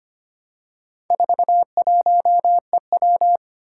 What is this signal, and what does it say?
Morse code, heard as a keyed tone, about 25 wpm, 700 Hz. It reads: 41EW
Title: Four Challenging Letter-Number Combinations 25wpm